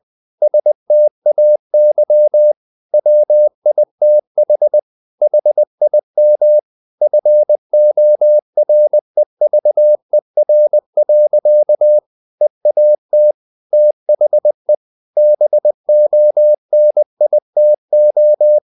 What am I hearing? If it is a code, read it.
Morse code, heard as a keyed tone, about 20 wpm, 600 Hz. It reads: STAY WITH HIM FOREVER. EAT THE BONITO
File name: the old man and the sea / the old man and the sea part 1 20wpm